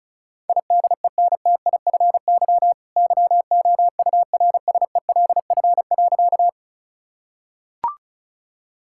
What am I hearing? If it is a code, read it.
Morse code, heard as a keyed tone, about 36 wpm, 690 Hz. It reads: IDENTIFY YOURSELF.
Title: Sentences from Top 1000 Words - Review Part 6 - 35wpm